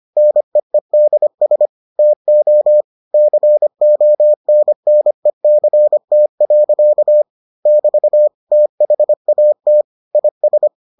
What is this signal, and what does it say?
Morse code, heard as a keyed tone, about 25 wpm, 600 Hz. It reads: NEEDS TO CONNECT. <BT> THAT IS